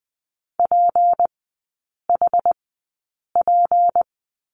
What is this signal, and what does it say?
Morse code, heard as a keyed tone, about 20 wpm, spaced as for 8 wpm, 700 Hz. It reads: PHP
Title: 191 Lesson 31e - Letter v - Three Characters 20wpm